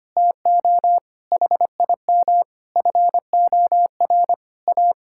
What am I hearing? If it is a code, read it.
Morse code, heard as a keyed tone, about 25 wpm, 700 Hz. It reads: TO HIM FOR A
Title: Sentences from Top 1600 Words 25wpm